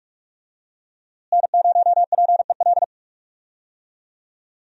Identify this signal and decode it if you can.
Morse code, heard as a keyed tone, about 45 wpm, 700 Hz. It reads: N0PEL
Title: USA Call Signs - 5 Characters - Repeated 3x - 45wpm